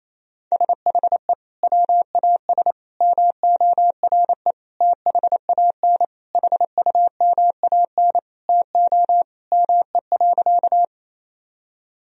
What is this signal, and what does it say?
Morse code, heard as a keyed tone, about 28 wpm, 700 Hz. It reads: SHE WAS MORE THAN HUMAN TO ME.
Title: Sentences from Top 1000 Words - Review Part 3 - 28wpm